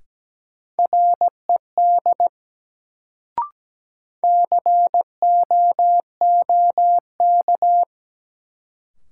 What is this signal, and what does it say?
Morse code, heard as a keyed tone, about 17 wpm, 700 Hz. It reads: RED COOK
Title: Sentences from Top 700 Words - New Word Drill - 17wpm